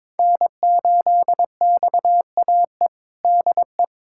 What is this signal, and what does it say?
Morse code, heard as a keyed tone, about 22 wpm, 700 Hz. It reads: N8XAE DE
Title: USA Callsign DE Callsign - 5 Characters - 22wpm